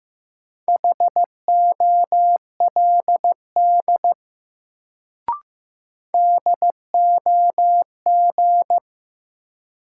Morse code, heard as a keyed tone, about 16 wpm, 700 Hz. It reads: HOLD DOG
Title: Sentences from Top 400 Words - New Word Drill - 15wpm